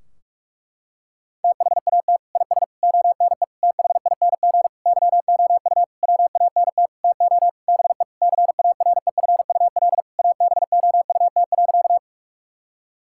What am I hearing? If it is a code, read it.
Morse code, heard as a keyed tone, about 41 wpm, 700 Hz. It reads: THAT IS ONE THING YOU WANT TO BE CAREFUL ABOUT.
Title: Sentences from Top 1600 Words 45wpm